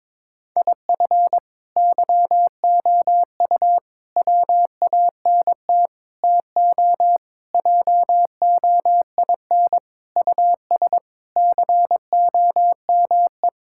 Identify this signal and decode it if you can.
Morse code, heard as a keyed tone, about 22 wpm, 700 Hz. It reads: IF YOU WANT TO JOIN US COME
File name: practice / 22wpm / Sentences with 4-Word Phrases - 22wpm